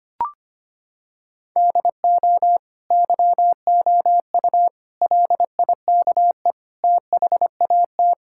Morse code, heard as a keyed tone, about 25 wpm, 700 Hz. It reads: DO YOU LIKE THAT
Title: Sentences from Top 1000 Words - Review Part 5 - 25wpm